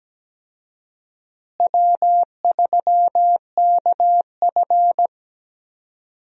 Morse code, heard as a keyed tone, about 17 wpm, 700 Hz. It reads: W3KF
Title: USA Call Signs - 4 Characters - Repeated 3x - 17wpm